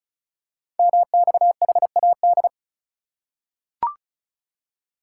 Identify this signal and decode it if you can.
Morse code, heard as a keyed tone, about 36 wpm, 700 Hz. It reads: MXHAD
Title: Code Groups - 5 Letters - 35wpm